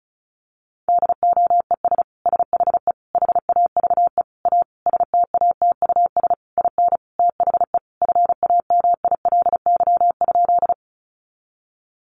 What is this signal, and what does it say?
Morse code, heard as a keyed tone, about 34 wpm, 700 Hz. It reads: DOES SHE HAVE A STATUS IN THE FAMILY?
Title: Sentences from Top 1400 Words 35wpm